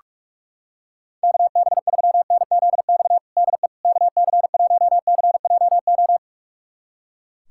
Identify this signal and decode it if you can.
Morse code, heard as a keyed tone, about 41 wpm, 700 Hz. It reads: KB3NZX DE KC1CJO
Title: USA Callsign DE Callsign 45wpm